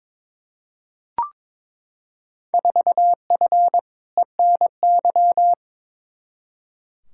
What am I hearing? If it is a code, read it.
Morse code, heard as a keyed tone, about 21 wpm, 700 Hz. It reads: 4F ENY
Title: Field Day Exchanges - 22wpm